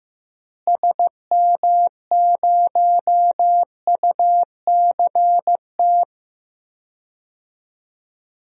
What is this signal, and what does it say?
Morse code, heard as a keyed tone, about 15 wpm, 700 Hz. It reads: SM0UCT
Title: DXCC-100 Call Signs - 6 Characters - Repeated 3x - 15wpm